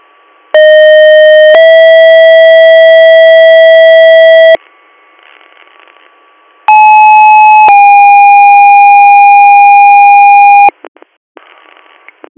Two-tone page, heard as a one-second tone, then a three-second tone, about 620 then 650 Hz, about 860 then 810 Hz. It is heard from 0.5 to 10.7 s.